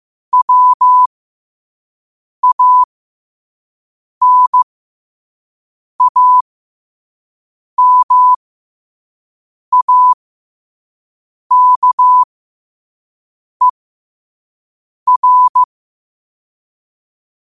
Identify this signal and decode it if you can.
Morse code, heard as a keyed tone, about 15 wpm, spaced as for 5 wpm, 1000 Hz. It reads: WANAMAKER
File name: qotdfiles / 5wpm